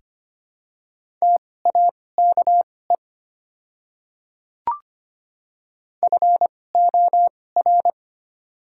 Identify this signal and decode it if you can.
Morse code, heard as a keyed tone, about 25 wpm, spaced as for 18 wpm, 690 Hz. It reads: TAKE FOR
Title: 2x Character Spacing - Top 200 Words - 25wpm